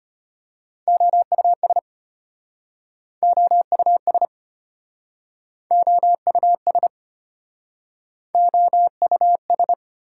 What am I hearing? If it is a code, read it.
Morse code, heard as a keyed tone, about 32 wpm, 690 Hz. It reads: OUS OUS OUS OUS